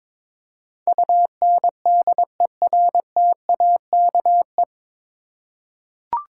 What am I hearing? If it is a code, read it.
Morse code, heard as a keyed tone, about 21 wpm, 700 Hz. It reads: UNDERTAKE
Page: Morse Code Ninja practice 22wpm